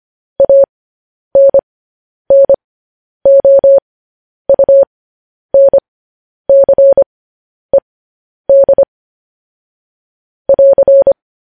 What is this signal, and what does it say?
Morse code, heard as a keyed tone, about 25 wpm, spaced as for 10 wpm, 550 Hz. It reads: ANNOUNCED <AR>
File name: podcast / News headlines-1x-25x10